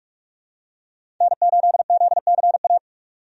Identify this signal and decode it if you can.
Morse code, heard as a keyed tone, about 43 wpm, 700 Hz. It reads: N8ZCA
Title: USA Call Signs - 5 Characters - 45wpm